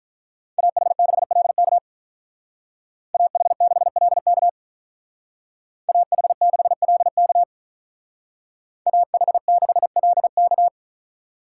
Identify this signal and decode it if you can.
Morse code, heard as a keyed tone, about 43 wpm, 700 Hz. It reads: AH6LK AH6LK AH6LK AH6LK
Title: Speed Racing - USA 2x2 Call Signs 35wpm